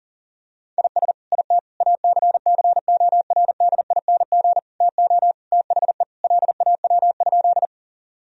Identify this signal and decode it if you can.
Morse code, heard as a keyed tone, about 40 wpm, 700 Hz. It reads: IS IT ACCORDING TO THE LAW?
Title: Sentences from Top 1000 Words - Review Part 4 - 40wpm